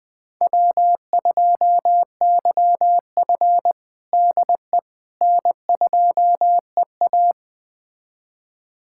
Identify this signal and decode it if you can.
Morse code, heard as a keyed tone, about 20 wpm, 700 Hz. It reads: W2YF DE N2EA
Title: USA Callsign DE Callsign - 4 Characters - 20wpm